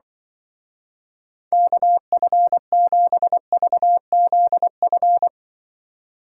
Morse code, heard as a keyed tone, about 23 wpm, 700 Hz. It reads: KF7VZF
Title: Speed Racing - USA Call Signs 20wpm